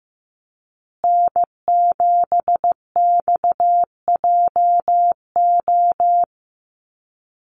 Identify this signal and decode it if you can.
Morse code, heard as a keyed tone, about 15 wpm, 700 Hz. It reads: N7XJO
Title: USA Call Signs Repeated 2x 15wpm